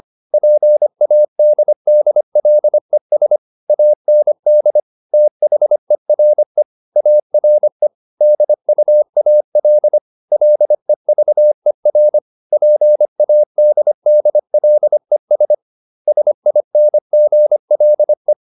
Morse code, heard as a keyed tone, about 25 wpm, 600 Hz. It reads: PADDLES AND THERE ARE DUAL LEVER PADDLES SINGLE